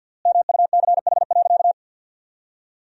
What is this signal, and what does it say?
Morse code, heard as a keyed tone, about 50 wpm, 700 Hz. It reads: MUCH.
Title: 3x Word-Spacing - Sentences from Top 400 Words - 50wpm